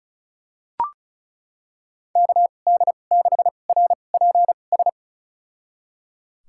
Morse code, heard as a keyed tone, about 34 wpm, spaced as for 25 wpm, 700 Hz. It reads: KD6RPS
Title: USA Call Signs - 6 Characters - 2x Character Spacing - 35wpm